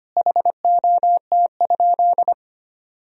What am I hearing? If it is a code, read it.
Morse code, heard as a keyed tone, about 25 wpm, 700 Hz. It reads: HOT?
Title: Sentences from Top 1000 Words - Review Part 3 - 25wpm